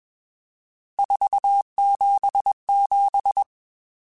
Morse code, heard as a keyed tone, about 21 wpm, 800 Hz. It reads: 477